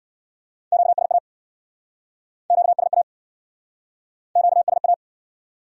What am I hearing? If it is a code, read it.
Morse code, heard as a keyed tone, about 69 wpm, 700 Hz. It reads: QSA QSA QSA